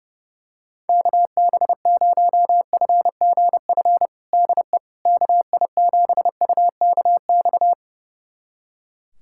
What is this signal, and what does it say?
Morse code, heard as a keyed tone, about 30 wpm, 700 Hz. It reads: KB0FGF DE KI7UKX
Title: USA Callsign DE Callsign - 6 Characters - 30wpm